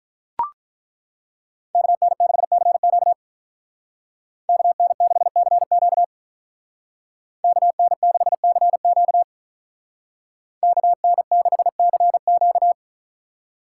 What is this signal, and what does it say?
Morse code, heard as a keyed tone, about 43 wpm, 700 Hz. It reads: KN6CQ KN6CQ KN6CQ KN6CQ
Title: Speed Racing - USA 2x2 Call Signs 35wpm